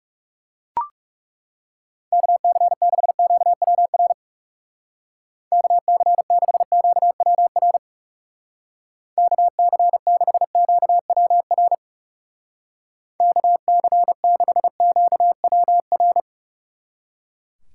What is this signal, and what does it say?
Morse code, heard as a keyed tone, about 37 wpm, 700 Hz. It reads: KC6QWR KC6QWR KC6QWR KC6QWR
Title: Speed Racing - USA 2x3 Call Signs 30wpm